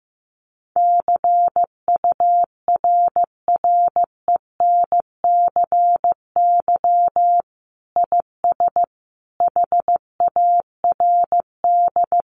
Morse code, heard as a keyed tone, about 15 wpm, 700 Hz. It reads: CURRENCY IS HARD
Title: Sentences from Top 1500 Words 15wpm